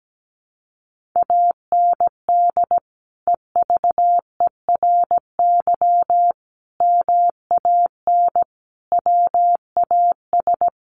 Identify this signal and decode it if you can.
Morse code, heard as a keyed tone, about 17 wpm, 700 Hz. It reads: AND EVERY MAN WAS
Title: Sentences from Top 1500 Words 17wpm